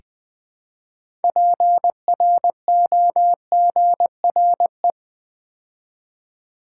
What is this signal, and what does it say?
Morse code, heard as a keyed tone, about 20 wpm, 700 Hz. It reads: PROGRE
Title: Build A Word - 2 Letters At A Time - 8-Letter Words - 20wpm